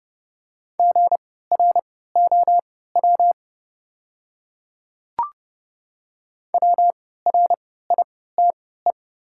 Morse code, heard as a keyed tone, about 31 wpm, spaced as for 17 wpm, 690 Hz. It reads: GROW WRITE
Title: 3x Character Spacing - Top 500 Words - 30wpm